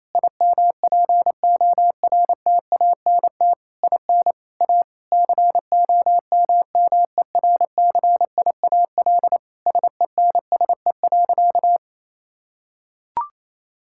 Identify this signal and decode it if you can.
Morse code, heard as a keyed tone, about 28 wpm, 700 Hz. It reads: IMPORTANT IN A COMMERCIAL SENSE.
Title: Sentences from Top 1000 Words - Review Part 5 - 28wpm